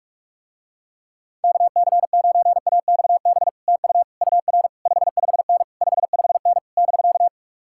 Morse code, heard as a keyed tone, about 49 wpm, 700 Hz. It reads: KC9AXB TU UR 55N 55N <BK>